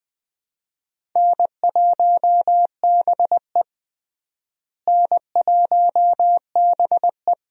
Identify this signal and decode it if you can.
Morse code, heard as a keyed tone, about 20 wpm, 700 Hz. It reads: N1BE N1BE
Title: USA Call Signs - 4 Characters - Repeated 3x - 20wpm